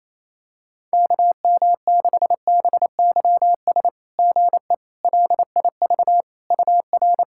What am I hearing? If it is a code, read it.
Morse code, heard as a keyed tone, about 28 wpm, 690 Hz. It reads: KM6BYS GE LIV UR
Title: SOTA QSO Practice 28wpm